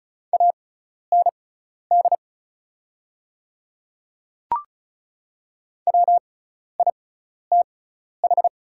Morse code, heard as a keyed tone, about 34 wpm, spaced as for 12 wpm, 700 Hz. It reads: AND WITH
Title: 6x Character Spacing - Top 100 Words - 35wpm